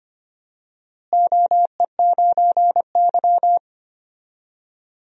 Morse code, heard as a keyed tone, about 25 wpm, 700 Hz. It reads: OE9Y
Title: DXCC-100 Call Signs - 4 Characters - Repeated 2x - 25wpm